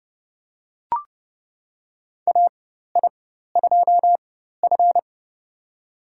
Morse code, heard as a keyed tone, about 30 wpm, spaced as for 14 wpm, 700 Hz. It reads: AI2F
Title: USA Call Signs - 4 Characters - 4x Character Spacing - 30wpm